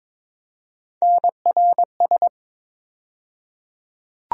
Morse code, heard as a keyed tone, about 21 wpm, 700 Hz. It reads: NRS
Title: USA Call Sign Postfixes - 3 Letters 22wpm